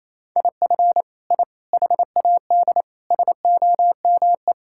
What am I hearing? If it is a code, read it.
Morse code, heard as a keyed tone, about 28 wpm, 700 Hz. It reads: IF I HAD SOME
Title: Sentences from Top 1000 Words - Review Part 5 - 28wpm